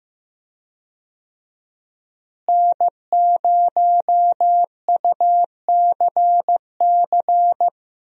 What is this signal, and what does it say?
Morse code, heard as a keyed tone, about 16 wpm, 700 Hz. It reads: N0UCC
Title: USA Call Signs - 5 Characters - Repeated 2x - 15wpm